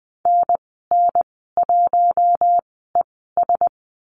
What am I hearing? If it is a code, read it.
Morse code, heard as a keyed tone, about 20 wpm, spaced as for 15 wpm, 700 Hz. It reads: NN1ES
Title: 042 Lesson 8g - Number 1 - Call Signs 20wpm